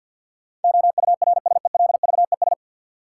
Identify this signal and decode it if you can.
Morse code, heard as a keyed tone, about 50 wpm, 700 Hz. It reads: OURSELVES
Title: Sentences from Top 1600 Words - New Word Drill Alt - 50wpm